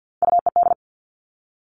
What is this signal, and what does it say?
Morse code, heard as a keyed tone, about 50 wpm, 700 Hz. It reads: UED
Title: Three Challenging Letter-Number Combinations 50wpm